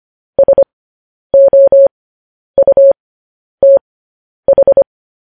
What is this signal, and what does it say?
Morse code, heard as a keyed tone, about 25 wpm, spaced as for 10 wpm, 550 Hz. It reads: SOUTH
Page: News Headlines in Morse Code at 10 WPM